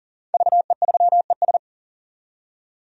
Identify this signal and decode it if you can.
Morse code, heard as a keyed tone, about 41 wpm, 700 Hz. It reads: VE3ES
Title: DXCC-100 Call Signs - 5 Characters - 40wpm